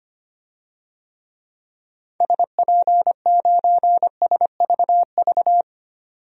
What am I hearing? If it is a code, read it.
Morse code, heard as a keyed tone, about 25 wpm, 700 Hz. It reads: SP9SVV
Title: DXCC-100 Call Signs - 6 Characters - 25wpm